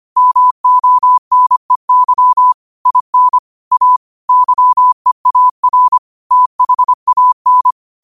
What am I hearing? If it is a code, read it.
Morse code, heard as a keyed tone, about 25 wpm, 1000 Hz. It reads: MONEY IN A YEAR THAN
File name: qotdfiles / 25wpm